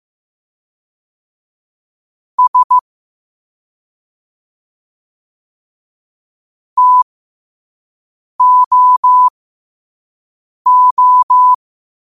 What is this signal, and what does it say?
Morse code, heard as a keyed tone, about 15 wpm, spaced as for 5 wpm, 1000 Hz. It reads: S TOO
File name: qotdfiles / 5wpm